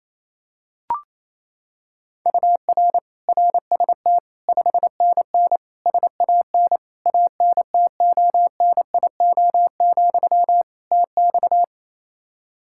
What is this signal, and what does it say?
Morse code, heard as a keyed tone, about 27 wpm, 700 Hz. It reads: UR RST 5NN SAN ANTONIO, TX
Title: Practice QSO - Exchange 1 - Beginning - 28wpm